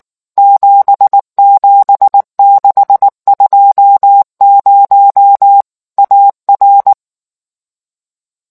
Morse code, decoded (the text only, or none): 77620 AR